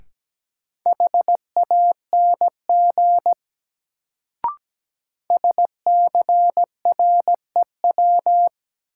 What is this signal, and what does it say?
Morse code, heard as a keyed tone, about 17 wpm, 700 Hz. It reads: HANG SCREW
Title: Sentences from Top 1600 Words - New Word Drill - 17wpm